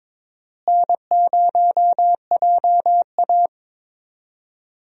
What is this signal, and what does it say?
Morse code, heard as a keyed tone, about 22 wpm, 700 Hz. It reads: N0JA
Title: USA Call Signs - 4 Characters - Repeated 2x - 22wpm